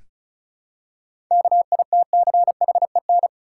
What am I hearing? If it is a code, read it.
Morse code, heard as a keyed tone, about 36 wpm, 700 Hz. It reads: KITCHEN
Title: Seven Letter Words - 35wpm